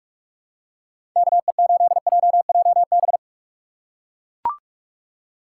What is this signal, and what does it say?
Morse code, heard as a keyed tone, about 42 wpm, 700 Hz. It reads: KE8JJB
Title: USA Call Signs - 6 Characters - 2x Character Spacing - 45wpm